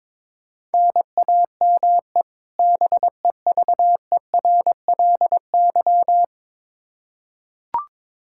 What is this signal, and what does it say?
Morse code, heard as a keyed tone, about 22 wpm, 700 Hz. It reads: NAME BEVERLY
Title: Practice QSO - Exchange 1 - Beginning - 22wpm